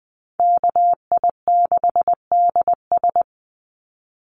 Morse code, heard as a keyed tone, about 20 wpm, 700 Hz. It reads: KI6DS
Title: NA SOTA Chaser Callsigns 20wpm